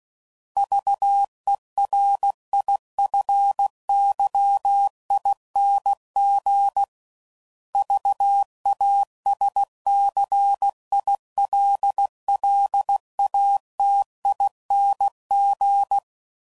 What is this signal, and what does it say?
Morse code, heard as a keyed tone, about 16 wpm, 800 Hz. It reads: VERIFYING VASCILLATING
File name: Suffixes / Special INGs / Sngl Ings